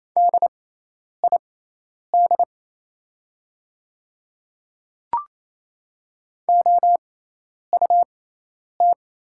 Morse code, heard as a keyed tone, about 28 wpm, spaced as for 10 wpm, 700 Hz. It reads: DID OUT
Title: 6x Character Spacing - Top 200 Words - 28wpm